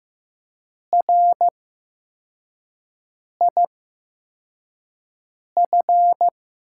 Morse code, heard as a keyed tone, about 15 wpm, spaced as for 4 wpm, 700 Hz. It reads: RIF
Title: USA Call Signs - 6 Characters - 8x Character Spacing - 15wpm